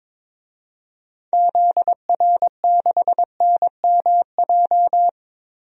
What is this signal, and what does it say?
Morse code, heard as a keyed tone, about 22 wpm, 700 Hz. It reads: ZR6NMJ
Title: DXCC-100 Call Signs - 6 Characters - 22wpm